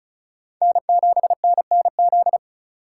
Morse code, heard as a keyed tone, about 36 wpm, 700 Hz. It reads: N7NNZ
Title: USA Call Signs - 4x Character Spacing - 35wpm